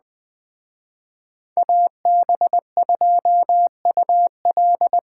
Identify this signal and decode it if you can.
Morse code, heard as a keyed tone, about 20 wpm, 700 Hz. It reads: AB2UL
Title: Speed Racing - USA 2x2 Call Signs 17wpm